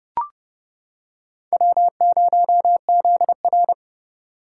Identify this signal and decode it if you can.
Morse code, heard as a keyed tone, about 31 wpm, 700 Hz. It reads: W0ZR